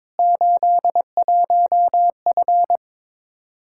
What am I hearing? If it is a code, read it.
Morse code, heard as a keyed tone, about 22 wpm, 700 Hz. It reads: 81F
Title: Practice QSO - Exchange 2 - Info - 22wpm